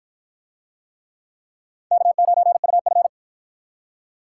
Morse code, heard as a keyed tone, about 53 wpm, 700 Hz. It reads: K9UF